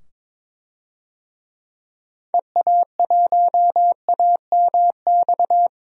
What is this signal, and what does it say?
Morse code, heard as a keyed tone, about 22 wpm, 700 Hz. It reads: EA1AMX
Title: DXCC-100 Call Signs - 6 Characters - 22wpm